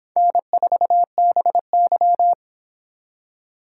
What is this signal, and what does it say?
Morse code, heard as a keyed tone, about 26 wpm, 700 Hz. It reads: N4BY